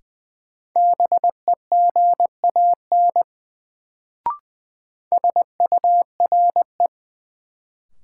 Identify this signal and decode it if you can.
Morse code, heard as a keyed tone, about 21 wpm, 700 Hz. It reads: BEGAN SURE
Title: Sentences from Top 300 Words - New Word Drill - 20wpm